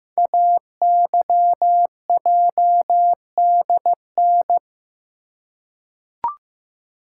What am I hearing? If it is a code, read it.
Morse code, heard as a keyed tone, about 15 wpm, 700 Hz. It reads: AYJDN
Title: Code Groups - 5 Letters - 15wpm